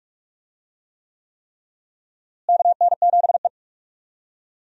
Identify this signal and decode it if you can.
Morse code, heard as a keyed tone, about 46 wpm, 700 Hz. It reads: KN7E
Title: USA Call Signs - 4 Characters - Repeated 3x - 45wpm